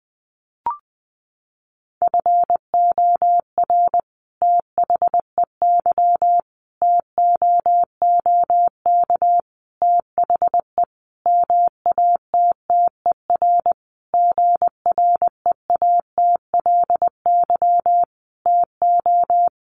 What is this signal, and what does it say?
Morse code, heard as a keyed tone, about 20 wpm, 700 Hz. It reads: FOR THEY TOOK THE MATTER GREATLY TO
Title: Sentences from Top 1400 Words 20wpm